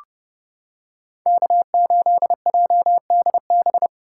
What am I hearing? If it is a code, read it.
Morse code, heard as a keyed tone, about 31 wpm, 700 Hz. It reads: K8JDB